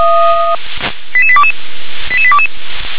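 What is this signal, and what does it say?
Five-tone selective call (ZVEI), heard as quick runs of five tones, tones of about 70 ms. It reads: 88822 88822
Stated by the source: Received by SDR, ZVEI five-tone call.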